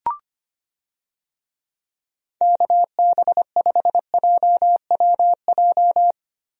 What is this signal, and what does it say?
Morse code, heard as a keyed tone, about 25 wpm, 700 Hz. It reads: KB5JWJ
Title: USA Call Signs - 6 Characters - Repeated 2x - 25wpm